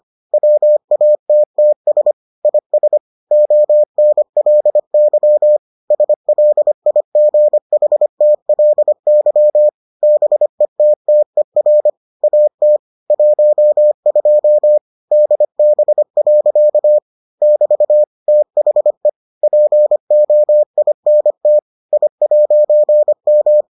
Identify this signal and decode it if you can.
Morse code, heard as a keyed tone, about 25 wpm, 600 Hz. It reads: WATTS IS ONLY SLIGHTLY BETTER AT 12 DB. <BT> THE POINT I'M